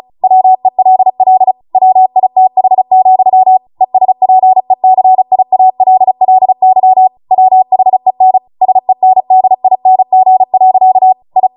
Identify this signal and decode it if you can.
Morse code, heard as a keyed tone, about 36 wpm, 750 Hz. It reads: WELL WITH, ESPECIALLY WHEN SENDING. I